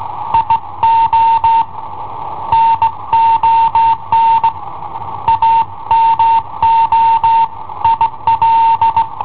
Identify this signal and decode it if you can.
Morse code, heard as a keyed tone, about 19 wpm, 910 Hz. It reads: IO NON AMO IL
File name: i7ohp / cassetta i7ohp B